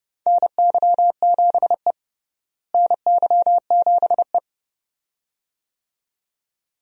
Morse code, heard as a keyed tone, about 30 wpm, 700 Hz. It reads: NY7E NY7E
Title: USA Call Signs - 4 Characters - Repeated 2x - 30wpm